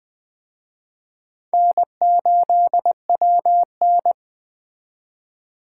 Morse code, heard as a keyed tone, about 21 wpm, 700 Hz. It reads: N8WN
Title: Speed Racing - USA 1x2 Call Signs 17wpm